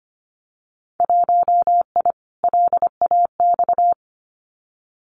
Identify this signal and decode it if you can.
Morse code, heard as a keyed tone, about 25 wpm, 690 Hz. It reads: 1I LAX
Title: Winter Field Day Exchanges 25wpm